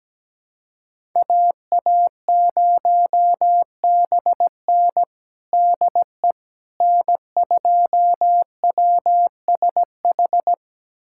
Morse code, heard as a keyed tone, about 17 wpm, 700 Hz. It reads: AA0BN DE N2WSH
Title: USA Callsign DE Callsign - 5 Characters - 17wpm